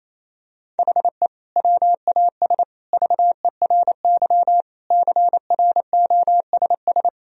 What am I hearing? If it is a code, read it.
Morse code, heard as a keyed tone, about 28 wpm, 700 Hz. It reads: HE WAS VERY CROSS
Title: Sentences from Top 1000 Words - Review Part 1 - 28wpm